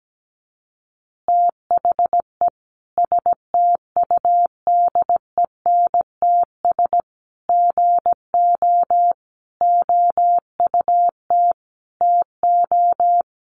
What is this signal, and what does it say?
Morse code, heard as a keyed tone, about 17 wpm, 700 Hz. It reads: THE STUDENTS GO OUT TO
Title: Easy Phrases 17wpm